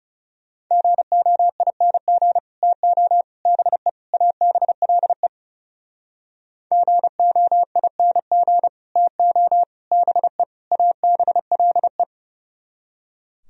Morse code, heard as a keyed tone, about 37 wpm, 700 Hz. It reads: GOING TO BE ABLE GOING TO BE ABLE
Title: Speed-Racing - Sets of 4 Words 30wpm